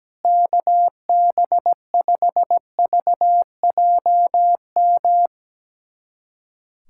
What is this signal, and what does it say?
Morse code, heard as a keyed tone, about 17 wpm, 700 Hz. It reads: KB5VJM
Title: USA Callsign DE Callsign 17wpm